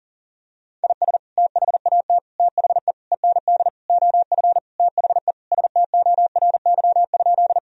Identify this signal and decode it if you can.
Morse code, heard as a keyed tone, about 40 wpm, 700 Hz. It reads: IS THAT THE END OF THE STORY?
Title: Sentences from Top 1000 Words - Review Part 1 - 40wpm